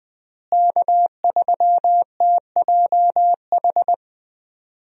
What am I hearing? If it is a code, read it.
Morse code, heard as a keyed tone, about 20 wpm, 700 Hz. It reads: K3TJH
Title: USA Call Signs - 5 Characters - Repeated 3x - 20wpm